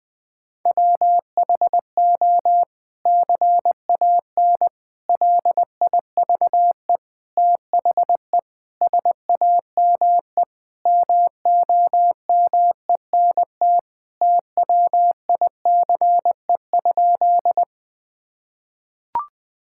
Morse code, heard as a keyed tone, about 21 wpm, 700 Hz. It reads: WHO CAN LIVE THE SAME MOMENT TWICE?
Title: Sentences from Top 1600 Words 20wpm